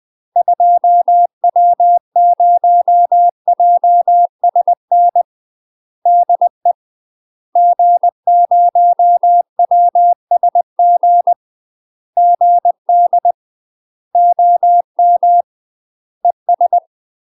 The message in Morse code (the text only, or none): 2W0JSN DE G0WSG GD OM ES